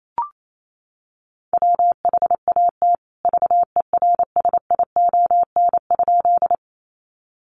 Morse code, heard as a keyed tone, about 28 wpm, 700 Hz. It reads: WHAT VERSION?
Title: Sentences from Top 1400 Words 28wpm